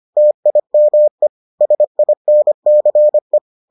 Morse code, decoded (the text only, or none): TIME SINCE